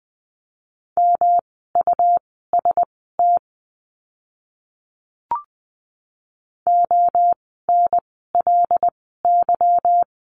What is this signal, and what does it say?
Morse code, heard as a keyed tone, about 20 wpm, spaced as for 15 wpm, 700 Hz. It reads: MUST ONLY
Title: 237 Lesson 38f - Letter q - Words 20wpm